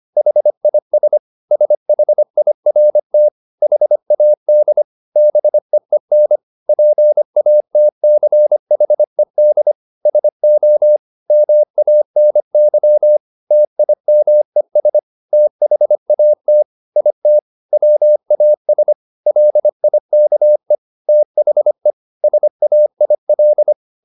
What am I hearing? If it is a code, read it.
Morse code, heard as a keyed tone, about 25 wpm, 600 Hz. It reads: HIS SHIRT HAD BEEN PATCHED SO MANY TIMES THAT IT WAS LIKE THE SAIL
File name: the old man and the sea part 1 25wpm